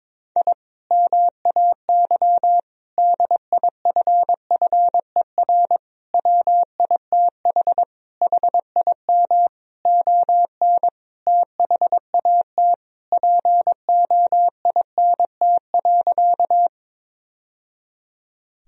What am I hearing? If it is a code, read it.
Morse code, heard as a keyed tone, about 22 wpm, 700 Hz. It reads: I MAY DIFFER WITH HIM ON THAT POINT.
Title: Sentences from Top 1000 Words - Review Part 1 - 22wpm